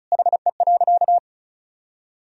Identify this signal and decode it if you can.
Morse code, heard as a keyed tone, about 34 wpm, 700 Hz. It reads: HE.
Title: Sentences from Top 100 Words and Abbreviations 35wpm